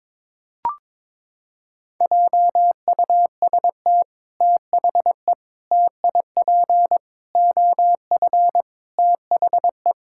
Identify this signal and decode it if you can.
Morse code, heard as a keyed tone, about 22 wpm, 700 Hz. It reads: JUST THE TIP OF THE
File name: practice / 22wpm / Sets of 6 Words - 22wpm